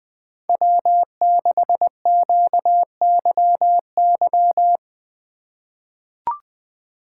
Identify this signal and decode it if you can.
Morse code, heard as a keyed tone, about 20 wpm, 700 Hz. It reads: W6QYY
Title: USA Call Signs - 3x Character Spacing - 20wpm